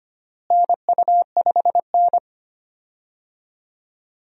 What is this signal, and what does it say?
Morse code, heard as a keyed tone, about 25 wpm, 700 Hz. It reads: NU5N
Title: USA Call Signs - 4 Characters - Repeated 3x - 25wpm